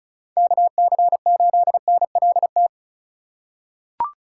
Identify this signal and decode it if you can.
Morse code, heard as a keyed tone, about 36 wpm, 700 Hz. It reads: KC8NLT
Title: USA Call Signs - 6 Characters - 6x Character Spacing - 35wpm